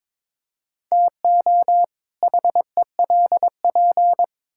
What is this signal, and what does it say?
Morse code, heard as a keyed tone, about 21 wpm, 700 Hz. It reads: TO HELP
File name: Build A Phrase - 2-Word Phrases - 22wpm